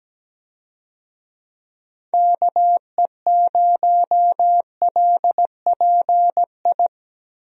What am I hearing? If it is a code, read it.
Morse code, heard as a keyed tone, about 17 wpm, 700 Hz. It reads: KE0LPI
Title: USA Call Signs - 6 Characters - Repeated 3x - 17wpm